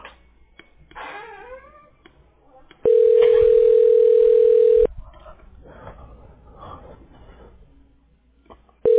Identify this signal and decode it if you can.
Ringback tone (North America), 440 + 480 Hz.